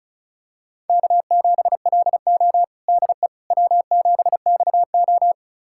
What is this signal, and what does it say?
Morse code, heard as a keyed tone, about 36 wpm, 700 Hz. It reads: K7LO DE W7XO
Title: USA Callsign DE Callsign - 4 Characters - 35wpm